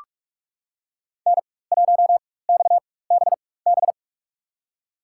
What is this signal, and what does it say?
Morse code, heard as a keyed tone, about 46 wpm, spaced as for 21 wpm, 700 Hz. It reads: N1XBB
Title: USA Call Signs - 4x Character Spacing - 45wpm